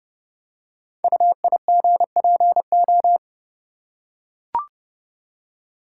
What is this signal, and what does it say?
Morse code, heard as a keyed tone, about 30 wpm, 690 Hz. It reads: UIGPO
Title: Code Groups - 5 Letters - 30wpm